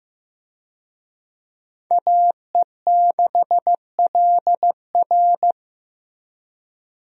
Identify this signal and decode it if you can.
Morse code, heard as a keyed tone, about 15 wpm, 700 Hz. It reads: AE6LR
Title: USA Call Signs - 5 Characters - Repeated 2x - 15wpm